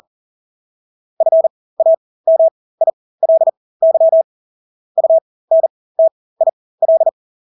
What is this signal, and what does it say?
Morse code, heard as a keyed tone, about 40 wpm, spaced as for 20 wpm, 650 Hz. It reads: FAMILY UNTIL
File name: Engelske ord 40-20 wpm